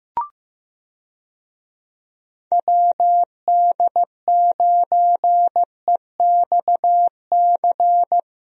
Morse code, heard as a keyed tone, about 15 wpm, 700 Hz. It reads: WD9EXC
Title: USA Call Signs - 6 Characters - Repeated 3x - 15wpm